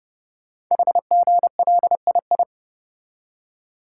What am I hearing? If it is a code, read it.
Morse code, heard as a keyed tone, about 30 wpm, 700 Hz. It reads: HGLII
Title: Code Groups - 5 Letters - 30wpm